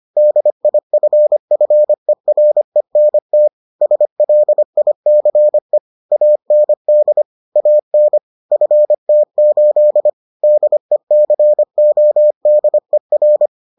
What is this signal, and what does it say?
Morse code, heard as a keyed tone, about 25 wpm, 600 Hz. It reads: DIFFERENT SLICE AND AN FT8 DECODER